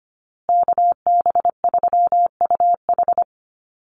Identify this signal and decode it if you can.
Morse code, heard as a keyed tone, about 25 wpm, 700 Hz. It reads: KB3UH
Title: USA Call Signs Repeated 3x 25wpm